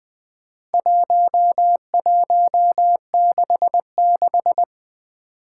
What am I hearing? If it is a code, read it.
Morse code, heard as a keyed tone, about 20 wpm, 700 Hz. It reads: 1166
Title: Four Challenging Letter-Number Combinations 20wpm